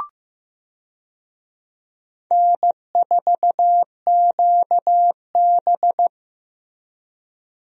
Morse code, heard as a keyed tone, about 15 wpm, 700 Hz. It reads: N4QB
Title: USA Call Signs - 4 Characters - Repeated 2x - 15wpm